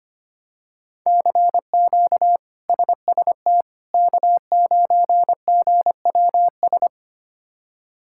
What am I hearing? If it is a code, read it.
Morse code, heard as a keyed tone, about 25 wpm, 700 Hz. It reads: CQ SST K9GWS
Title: K1USN SST QSO Practice - 25wpm